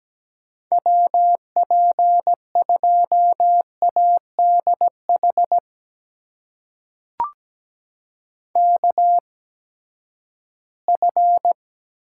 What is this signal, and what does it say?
Morse code, heard as a keyed tone, about 17 wpm, spaced as for 12 wpm, 700 Hz. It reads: WP2ADH KF
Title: USA Call Signs - 6 Characters - 8x Character Spacing - 17wpm